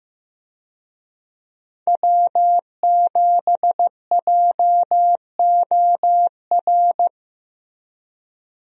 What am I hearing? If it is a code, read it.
Morse code, heard as a keyed tone, about 14 wpm, 700 Hz. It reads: W7JOR
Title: USA Call Signs - 5 Characters - Repeated 2x - 15wpm